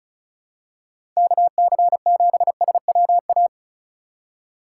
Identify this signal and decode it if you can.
Morse code, heard as a keyed tone, about 36 wpm, 700 Hz. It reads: KC7SWA
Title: USA Call Signs - 6 Characters - 3x Character Spacing - 35wpm